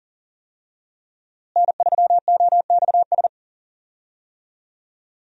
Morse code, heard as a keyed tone, about 40 wpm, 700 Hz. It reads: N3OXS
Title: USA Call Signs - 5 Characters - Repeated 2x - 40wpm